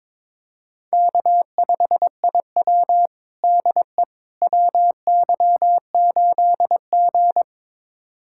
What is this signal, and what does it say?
Morse code, heard as a keyed tone, about 22 wpm, 700 Hz. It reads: K5IW DE WY8G
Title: USA Callsign DE Callsign - 4 Characters - 22wpm